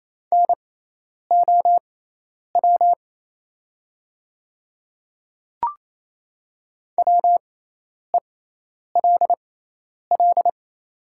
Morse code, heard as a keyed tone, about 27 wpm, spaced as for 10 wpm, 700 Hz. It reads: NOW WELL
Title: 6x Character Spacing - Top 200 Words - 28wpm